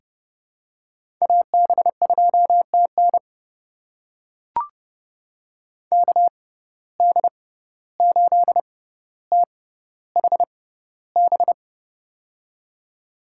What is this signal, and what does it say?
Morse code, heard as a keyed tone, about 30 wpm, spaced as for 15 wpm, 700 Hz. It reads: AB2TN KD8THB
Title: USA Call Signs - 6x Character Spacing - 30wpm